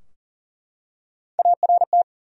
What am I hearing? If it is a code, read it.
Morse code, heard as a keyed tone, about 37 wpm, 700 Hz. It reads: ART
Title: USA Common Names - Max 3 Letters - 40wpm